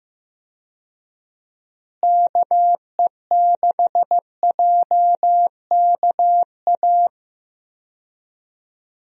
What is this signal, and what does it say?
Morse code, heard as a keyed tone, about 15 wpm, 700 Hz. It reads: KE6JKA
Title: USA Call Signs - 6 Characters - Repeated 2x - 15wpm